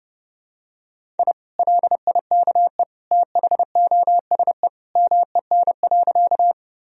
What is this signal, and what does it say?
Morse code, heard as a keyed tone, about 30 wpm, 700 Hz. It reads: I LIKE THOSE MEN.
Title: Sentences from Top 1000 Words - Review Part 1 - 30wpm